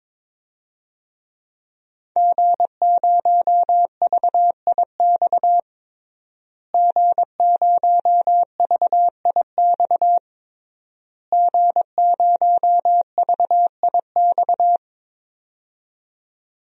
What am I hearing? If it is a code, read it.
Morse code, heard as a keyed tone, about 21 wpm, 700 Hz. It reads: G0VIX G0VIX G0VIX
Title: DXCC-100 Call Signs - 5 Characters - Repeated 3x - 22wpm